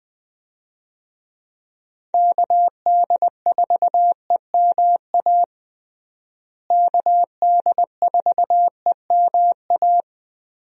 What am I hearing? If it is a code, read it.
Morse code, heard as a keyed tone, about 20 wpm, 700 Hz. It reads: KD4EMA KD4EMA
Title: USA Call Signs - 6 Characters - Repeated 2x - 20wpm